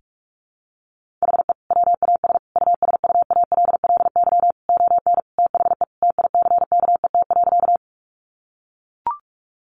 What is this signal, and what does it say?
Morse code, heard as a keyed tone, about 44 wpm, 700 Hz. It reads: HE WAS USUALLY ON THE TICKET.
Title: Sentences from Top 1400 Words 45wpm